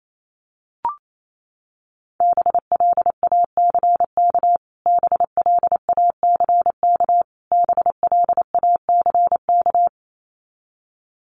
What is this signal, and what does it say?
Morse code, heard as a keyed tone, about 28 wpm, 700 Hz. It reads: BLACK BLACK BLACK
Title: Top 500 Words Repeated 28wpm